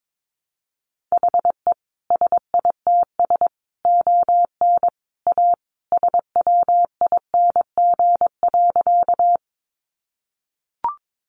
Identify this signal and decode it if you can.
Morse code, heard as a keyed tone, about 22 wpm, 700 Hz. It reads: HE SITS ON A SWING.
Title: Additional Easy Phrases 22wpm